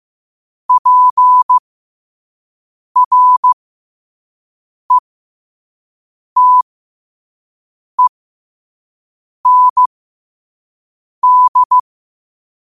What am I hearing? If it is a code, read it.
Morse code, heard as a keyed tone, about 15 wpm, spaced as for 5 wpm, 1000 Hz. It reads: PRETEND
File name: qotdfiles / 5wpm